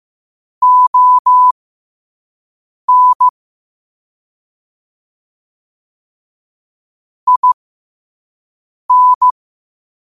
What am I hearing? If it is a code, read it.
Morse code, heard as a keyed tone, about 15 wpm, spaced as for 5 wpm, 1000 Hz. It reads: ON IN
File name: qotdfiles / 5wpm